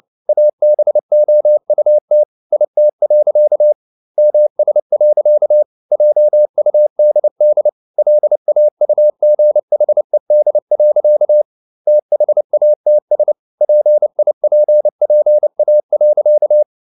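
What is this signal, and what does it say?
Morse code, heard as a keyed tone, about 29 wpm, 600 Hz. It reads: ABOUT IT. MS. JUDD LAUGHED. THATS PIPPA.
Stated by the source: Morse code at 29 wpm, sending Ken Follett's novel Never in English, from CW med Gnister.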